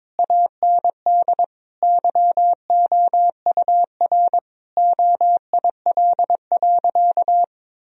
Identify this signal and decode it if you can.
Morse code, heard as a keyed tone, about 22 wpm, 700 Hz. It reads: AND YOUR OIL.
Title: Sentences from Top 1000 Words - Review Part 4 - 22wpm